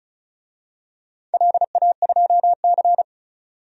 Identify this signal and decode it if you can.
Morse code, heard as a keyed tone, about 36 wpm, 700 Hz. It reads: LA2C
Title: DXCC-100 Call Signs - 4 Characters - 35wpm